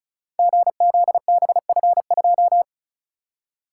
Morse code, heard as a keyed tone, about 36 wpm, 690 Hz. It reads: GZBF2
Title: Code Groups - 5 Letter:Numbers - 35wpm